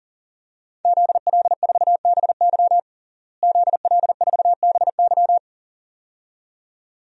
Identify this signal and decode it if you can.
Morse code, heard as a keyed tone, about 40 wpm, 700 Hz. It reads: ZL4BY ZL4BY
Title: DXCC-100 Call Signs - 5 Characters - Repeated 2x - 40wpm